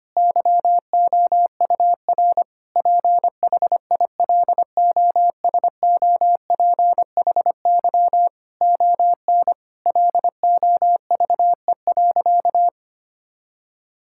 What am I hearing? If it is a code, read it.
Morse code, heard as a keyed tone, about 25 wpm, 700 Hz. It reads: YOUR PHILOSOPHY ON LOVE.
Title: Sentences from Top 1000 Words - Review Part 6 - 25wpm